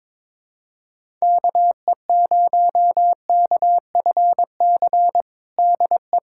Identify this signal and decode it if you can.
Morse code, heard as a keyed tone, about 22 wpm, 700 Hz. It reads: KE0KFC DE
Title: USA Callsign DE Callsign - 6 Characters - 22wpm